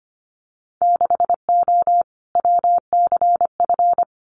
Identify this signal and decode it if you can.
Morse code, heard as a keyed tone, about 25 wpm, 700 Hz. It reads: 6O WCF
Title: Winter Field Day Exchanges Encore 25wpm